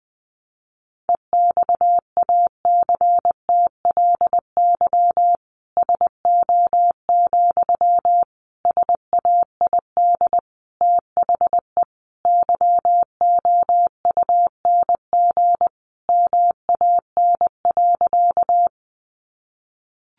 Morse code, heard as a keyed tone, about 20 wpm, 700 Hz. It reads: EXACTLY SO, SAID THE YOUNG MAN.
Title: Sentences from Top 900 Words 20wpm